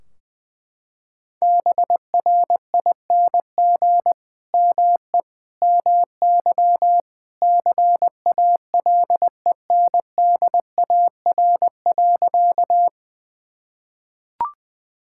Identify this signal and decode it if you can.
Morse code, heard as a keyed tone, about 21 wpm, 700 Hz. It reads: BRING ME MY CALENDAR.
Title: Sentences from Top 1600 Words 20wpm